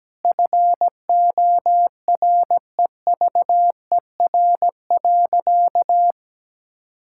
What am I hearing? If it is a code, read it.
Morse code, heard as a keyed tone, about 17 wpm, 700 Hz. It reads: FOREVER.
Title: Sentences from Top 1600 Words 17wpm